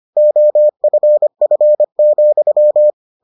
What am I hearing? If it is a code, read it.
Morse code, heard as a keyed tone, about 25 wpm, 600 Hz. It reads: OFF,